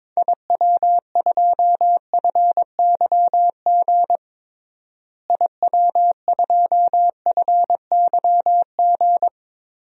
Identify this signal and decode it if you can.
Morse code, heard as a keyed tone, about 22 wpm, 700 Hz. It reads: IW2FYG IW2FYG
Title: DXCC-100 Call Signs - 6 Characters - Repeated 3x - 22wpm